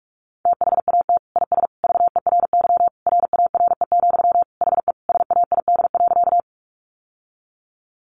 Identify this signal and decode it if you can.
Morse code, heard as a keyed tone, about 44 wpm, 700 Hz. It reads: THAT IS VERY RARE, HE SAID.
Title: Sentences from Top 1100 Words 45wpm